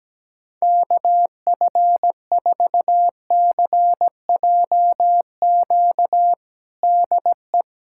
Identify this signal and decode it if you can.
Morse code, heard as a keyed tone, about 17 wpm, 700 Hz. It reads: KF4CJQ DE
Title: USA Callsign DE Callsign - 6 Characters - 17wpm